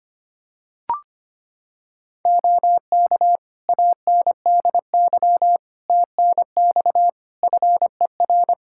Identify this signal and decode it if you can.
Morse code, heard as a keyed tone, about 25 wpm, 700 Hz. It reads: OK ANDY TNX FER
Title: K1USN SST QSO Practice 25wpm